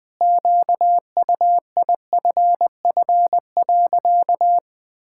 Morse code, heard as a keyed tone, about 20 wpm, 700 Hz. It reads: QUIFF.
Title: Pangram Challenge v2 - 20wpm